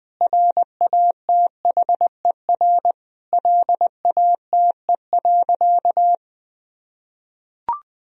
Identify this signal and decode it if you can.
Morse code, heard as a keyed tone, about 20 wpm, 700 Hz. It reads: RATHER LATE.
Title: Sentences from Top 1000 Words - Review Part 4 - 20wpm